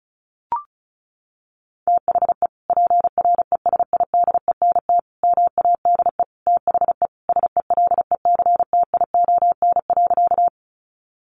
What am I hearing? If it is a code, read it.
Morse code, heard as a keyed tone, about 34 wpm, 700 Hz. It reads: THE PRESIDENT MADE THE SELECTION.
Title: Sentences from Top 1400 Words 35wpm